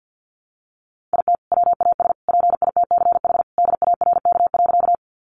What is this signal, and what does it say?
Morse code, heard as a keyed tone, about 50 wpm, 700 Hz. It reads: IT WAS PITCH DARK.
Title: Sentences from Top 1300 Words 50wpm